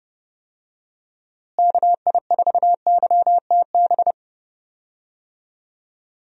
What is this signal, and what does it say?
Morse code, heard as a keyed tone, about 30 wpm, 700 Hz. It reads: KI4YTB
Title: USA Call Signs - 6 Characters - Repeated 2x - 30wpm